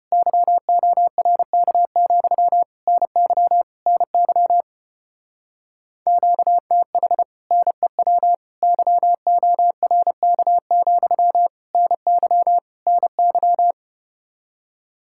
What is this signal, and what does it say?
Morse code, heard as a keyed tone, about 33 wpm, 700 Hz. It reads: YORK, NY NY QTH NEW YORK, NY NY